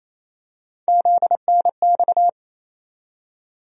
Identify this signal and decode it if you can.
Morse code, heard as a keyed tone, about 28 wpm, 700 Hz. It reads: ZNX
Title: USA Call Sign Postfixes - 3 Letters 28wpm